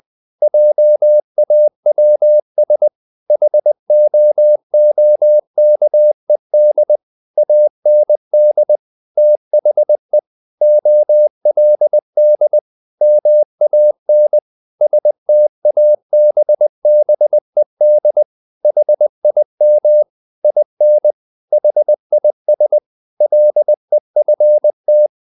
Morse code, heard as a keyed tone, about 20 wpm, 600 Hz. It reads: JAWS HOOKED AND THE OLD MAN STABBED HIM IN HIS LEFT
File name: the old man and the sea part 2 20wpm